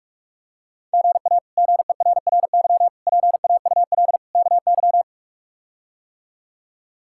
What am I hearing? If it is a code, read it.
Morse code, heard as a keyed tone, about 46 wpm, 700 Hz. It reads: GA GERRY PAUL KY
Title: K1USN SST QSO Practice - 45wpm